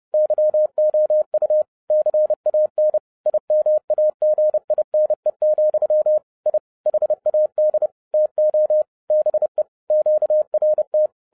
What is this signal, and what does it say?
Morse code, heard as a keyed tone, about 30 wpm, 610 Hz. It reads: YOU CAN IMAGINE, I HAD TO BE QRT
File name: 2011-07-mp3 / n40 EXCITING QRP CW OPERATION 30wpm